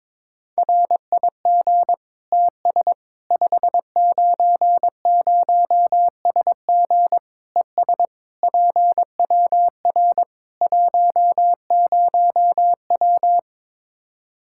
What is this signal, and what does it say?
Morse code, heard as a keyed tone, about 22 wpm, 700 Hz. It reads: RIG TS 590SG ES PWR 10W
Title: Practice QSO - Exchange 2 - Info - 22wpm